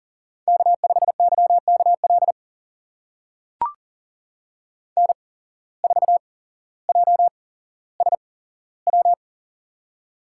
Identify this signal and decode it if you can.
Morse code, heard as a keyed tone, about 40 wpm, spaced as for 17 wpm, 700 Hz. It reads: K5YKL N4JSW
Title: USA Call Signs - 5 Characters - 8x Character Spacing - 40wpm